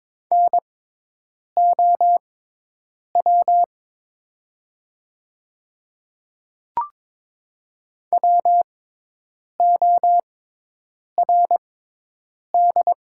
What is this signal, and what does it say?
Morse code, heard as a keyed tone, about 22 wpm, spaced as for 8 wpm, 700 Hz. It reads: NOW WORD
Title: 6x Character Spacing - Top 100 Words - 22wpm